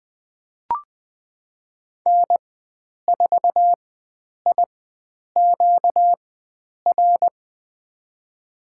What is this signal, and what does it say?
Morse code, heard as a keyed tone, about 20 wpm, spaced as for 9 wpm, 700 Hz. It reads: N4IQR
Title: USA Call Signs - 4x Character Spacing - 20wpm